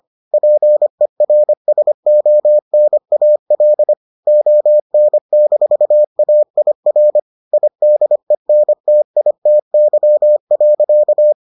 Morse code, heard as a keyed tone, about 25 wpm, 600 Hz. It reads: PERSONAL ON-AIR IDENTITY.